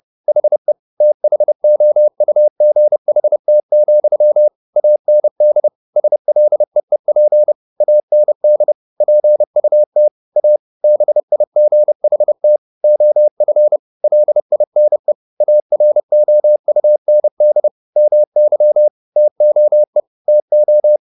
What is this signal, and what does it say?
Morse code, heard as a keyed tone, about 30 wpm, 600 Hz. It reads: HE THOUGHT, AND SLEEP AND PUT A BIGHT OF LINE AROUND MY TOE TO
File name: the old man and the sea / the old man and the sea 1 30wpm